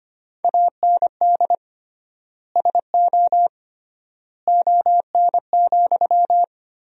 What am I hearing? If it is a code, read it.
Morse code, heard as a keyed tone, about 25 wpm, 700 Hz. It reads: AND SO ON,
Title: 3x Word-Spacing - Sentences from Top 100 Words - 25wpm